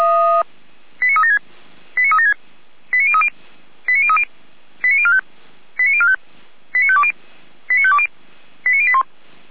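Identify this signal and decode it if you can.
Five-tone selective call (ZVEI), heard as quick runs of five tones, tones of about 80 ms. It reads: 89376 89376 89030 89030 89045 89045 89520 89520 89081
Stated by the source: ZVEI five-tone call received on an SDR.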